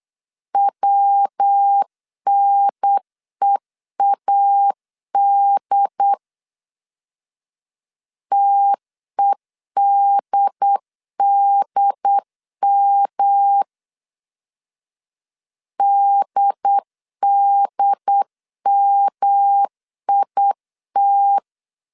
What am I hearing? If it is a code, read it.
Morse code, heard as a keyed tone, about 8 wpm, 800 Hz. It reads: WNEAD TEDDM DDMIT
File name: CW / DET / 1 3 1500C S C 8wpm 4214